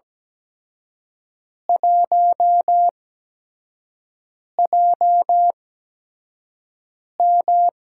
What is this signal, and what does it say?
Morse code, heard as a keyed tone, about 17 wpm, spaced as for 5 wpm, 700 Hz. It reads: 1JM
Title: USA Call Signs - 4 Characters - 8x Character Spacing - 17wpm